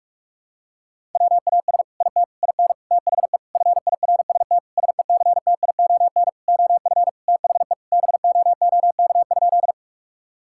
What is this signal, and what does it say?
Morse code, heard as a keyed tone, about 46 wpm, 700 Hz. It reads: WAS IT IN THE FIRST SECTION OF THE BOOK?
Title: Sentences from Top 1000 Words - Review Part 5 - 45wpm